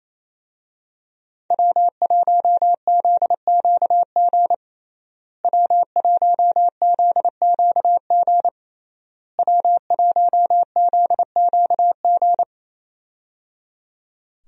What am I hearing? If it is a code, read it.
Morse code, heard as a keyed tone, about 27 wpm, 700 Hz. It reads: W1ZQG W1ZQG W1ZQG
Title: USA Call Signs - 5 Characters - Repeated 3x - 28wpm